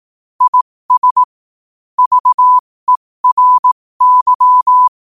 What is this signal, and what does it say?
Morse code, heard as a keyed tone, about 18 wpm, spaced as for 15 wpm, 1000 Hz. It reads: IS VERY
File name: qotdfiles / 15wpm